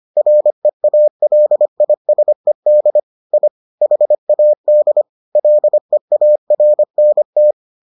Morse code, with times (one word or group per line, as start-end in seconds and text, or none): 0.2-3.0 REALISED
3.3-3.5 I
3.8-5.0 HAD
5.4-7.5 LEARNT